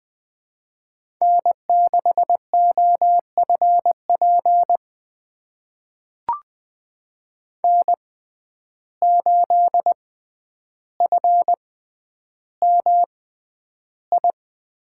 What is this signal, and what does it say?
Morse code, heard as a keyed tone, about 20 wpm, spaced as for 10 wpm, 700 Hz. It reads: N6OFP N8FMI
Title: USA Call Signs - 5 Characters - 6x Character Spacing - 20wpm